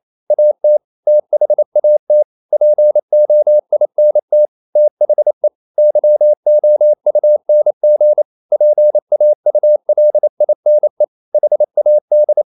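Morse code, heard as a keyed tone, about 28 wpm, 600 Hz. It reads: AT THAT POINT THE YOUNG PAULINE HAD
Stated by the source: Morse code practice at 28 wpm, text of Ken Follett's novel Never in English, from CW med Gnister.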